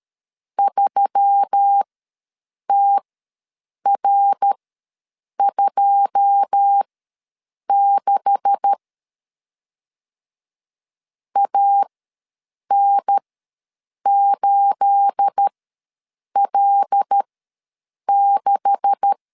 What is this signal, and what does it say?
Morse code, heard as a keyed tone, about 13 wpm, spaced as for 7 wpm, 800 Hz. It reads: 3TR26 AN8L6
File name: LEZIONI / 9 286 ws 12wpm 3351